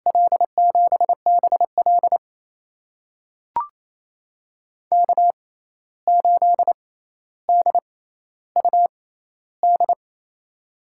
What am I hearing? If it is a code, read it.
Morse code, heard as a keyed tone, about 28 wpm, spaced as for 13 wpm, 700 Hz. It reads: L7BL K8DUD
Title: USA Call Signs - 5 Characters - 6x Character Spacing - 28wpm